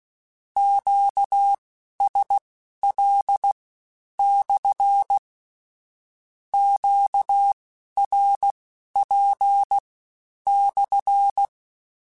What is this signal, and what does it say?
Morse code, heard as a keyed tone, about 16 wpm, spaced as for 10 wpm, 800 Hz. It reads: QSL/ QRP/
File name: Q Calls / Q Calls-mp3s / Q Calls SPSL1 @15